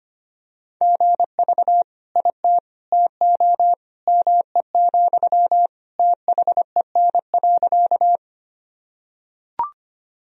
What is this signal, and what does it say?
Morse code, heard as a keyed tone, about 25 wpm, 700 Hz. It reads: GV IT TO ME, THEN.
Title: Sentences from Top 200 Words and Abbreviations 25wpm